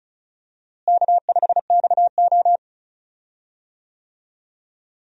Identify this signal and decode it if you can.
Morse code, heard as a keyed tone, about 34 wpm, 700 Hz. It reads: K5XO
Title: USA Call Signs - 4 Characters - Repeated 3x - 35wpm